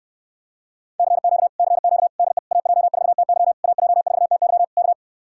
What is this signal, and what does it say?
Morse code, heard as a keyed tone, about 69 wpm, 700 Hz. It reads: CQ CQ DE AJ4TY AJ4TY K